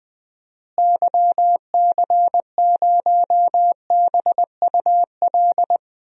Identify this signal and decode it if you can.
Morse code, heard as a keyed tone, about 20 wpm, 700 Hz. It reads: YC0BUL
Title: DXCC-100 Call Signs - 6 Characters - Repeated 3x - 20wpm